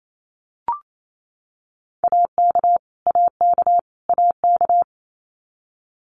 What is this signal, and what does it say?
Morse code, heard as a keyed tone, about 28 wpm, 700 Hz. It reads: AK AK AK
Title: US States Repeated 3x 28wpm